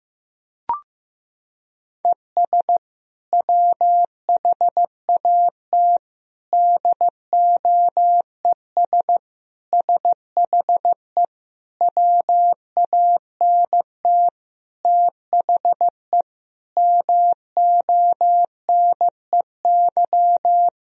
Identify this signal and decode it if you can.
Morse code, heard as a keyed tone, about 15 wpm, 700 Hz. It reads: ES WHAT DOES SHE WANT THE MONEY
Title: Sentences from Top 400 Words and Abbreviations 15wpm